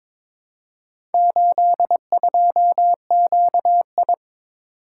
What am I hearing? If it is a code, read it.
Morse code, heard as a keyed tone, about 22 wpm, 700 Hz. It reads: 82QI
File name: practice / 22wpm / Four Letter-Number Combinations - 22wpm